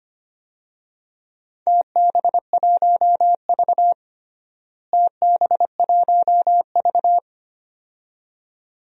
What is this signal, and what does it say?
Morse code, heard as a keyed tone, about 25 wpm, 700 Hz. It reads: TB1V TB1V
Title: DXCC-100 Call Signs - 4 Characters - Repeated 2x - 25wpm